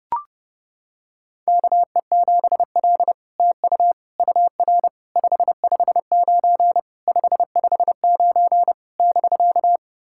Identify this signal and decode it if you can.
Morse code, heard as a keyed tone, about 30 wpm, 700 Hz. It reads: KE7L TU UR 559 559 <BK>
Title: SOTA QSO Practice 30wpm